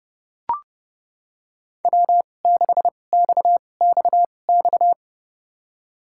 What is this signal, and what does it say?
Morse code, heard as a keyed tone, about 31 wpm, spaced as for 22 wpm, 690 Hz. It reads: W6XXX
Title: USA Call Signs - 5 Characters - 2x Character Spacing - 30wpm